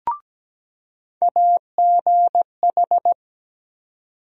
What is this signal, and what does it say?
Morse code, heard as a keyed tone, about 17 wpm, 700 Hz. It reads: AGH
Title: USA Call Sign Postfixes - 3 Letters 17wpm